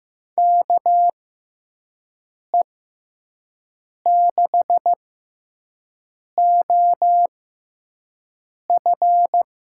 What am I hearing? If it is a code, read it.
Morse code, heard as a keyed tone, about 15 wpm, spaced as for 5 wpm, 700 Hz. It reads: KE6OF
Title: USA Call Signs - 6 Characters - 6x Character Spacing - 15wpm